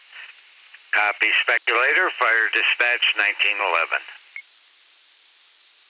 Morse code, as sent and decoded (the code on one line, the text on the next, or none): none